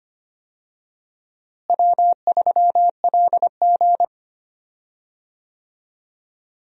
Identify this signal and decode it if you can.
Morse code, heard as a keyed tone, about 25 wpm, 700 Hz. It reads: W3LG
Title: USA Call Signs - 4 Characters - Repeated 3x - 25wpm